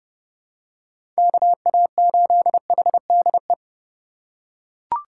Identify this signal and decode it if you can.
Morse code, heard as a keyed tone, about 30 wpm, 700 Hz. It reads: KA8HDE
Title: K1USN SST QSO Practice - 30wpm